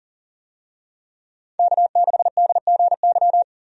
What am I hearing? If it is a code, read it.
Morse code, heard as a keyed tone, about 40 wpm, 700 Hz. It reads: K6DGY
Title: USA Call Signs - 5 Characters - 40wpm